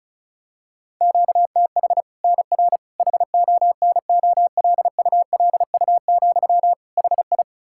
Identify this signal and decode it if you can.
Morse code, heard as a keyed tone, about 36 wpm, 700 Hz. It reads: QTH NR HONOLULU, HI
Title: Practice QSO - Exchange 1 - Beginning - 35wpm